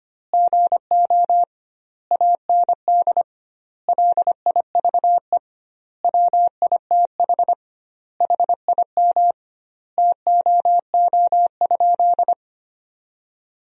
Morse code, heard as a keyed tone, about 25 wpm, 700 Hz. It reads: GO AND LIVE WITH HIM TOO?
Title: 2x Word-Spacing - Sentences from Top 500 Words - 25wpm